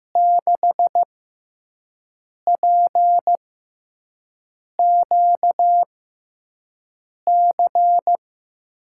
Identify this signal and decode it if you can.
Morse code, heard as a keyed tone, about 15 wpm, spaced as for 5 wpm, 700 Hz. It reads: 6PQC
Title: USA Call Signs - 6 Characters - 6x Character Spacing - 15wpm